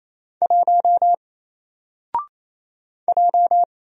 Morse code, heard as a keyed tone, about 29 wpm, 700 Hz. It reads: 1 J
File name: Single-Letter-Number-28wpm